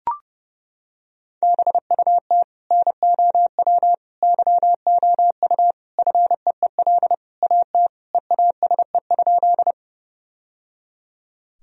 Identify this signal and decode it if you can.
Morse code, heard as a keyed tone, about 30 wpm, 700 Hz. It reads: BUT NOW YOU FEEL AT EASE?
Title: Sentences from Top 1000 Words - Review Part 1 - 30wpm